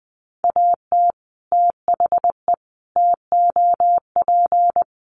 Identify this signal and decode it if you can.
Morse code, heard as a keyed tone, about 20 wpm, 700 Hz. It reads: AT THE TOP
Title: Sets of 4 Words 20wpm